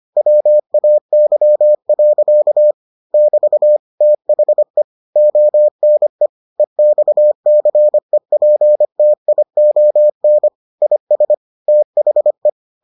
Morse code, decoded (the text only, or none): WAY. <BT> THE ONE EXCEPTION IS THE